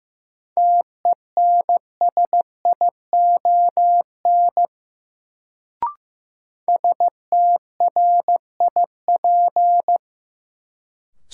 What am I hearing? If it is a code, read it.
Morse code, heard as a keyed tone, about 16 wpm, 700 Hz. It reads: TENSION STRIP
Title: Sentences from Top 1600 Words - New Word Drill Alt - 15wpm